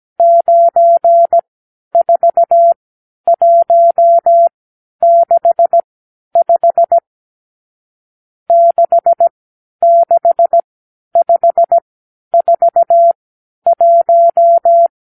Morse code, heard as a keyed tone, about 17 wpm, spaced as for 10 wpm, 680 Hz. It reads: 94165 66541
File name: arrl / Review (1-0)